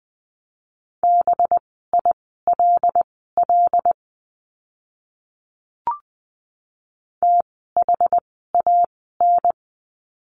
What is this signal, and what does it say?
Morse code, heard as a keyed tone, about 20 wpm, spaced as for 15 wpm, 700 Hz. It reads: BILL THAN
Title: 185 Lesson 30f - Letter b - Words 20wpm